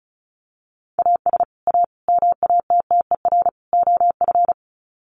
Morse code, heard as a keyed tone, about 34 wpm, 700 Hz. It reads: AS A MATTER OF
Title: Sets of 4 Words 35wpm